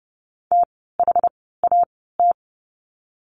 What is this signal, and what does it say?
Morse code, heard as a keyed tone, about 30 wpm, spaced as for 17 wpm, 700 Hz. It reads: THAT
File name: Lesson 16f - Letter c - Words 30wpm